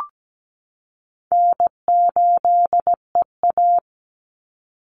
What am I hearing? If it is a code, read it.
Morse code, heard as a keyed tone, about 17 wpm, 700 Hz. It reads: N8EA
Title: Call Signs Repeated 2x - 17wpm